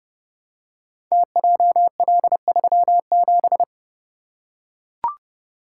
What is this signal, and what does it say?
Morse code, heard as a keyed tone, about 30 wpm, 700 Hz. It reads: TJL37
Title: Code Groups - 5 Letter:Numbers - 30wpm